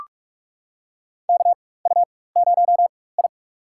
Morse code, heard as a keyed tone, about 42 wpm, spaced as for 20 wpm, 700 Hz. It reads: KU0I
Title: USA Call Signs - 4 Characters - 4x Character Spacing - 45wpm